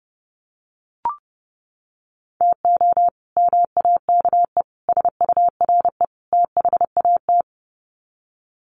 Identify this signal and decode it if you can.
Morse code, heard as a keyed tone, about 28 wpm, 700 Hz. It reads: TO MAKE SURE THAT
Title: Sets of 4 Words 30wpm